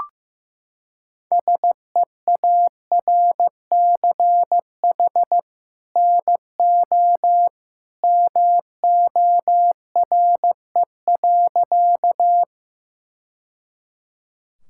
Morse code, heard as a keyed tone, about 15 wpm, 700 Hz. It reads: SEARCH NO MORE.
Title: Sentences from Top 1000 Words - Review Part 5 - 15wpm